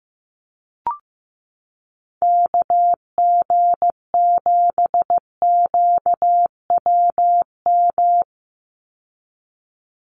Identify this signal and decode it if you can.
Morse code, heard as a keyed tone, about 15 wpm, 700 Hz. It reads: KG7QWM
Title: USA Call Signs 15wpm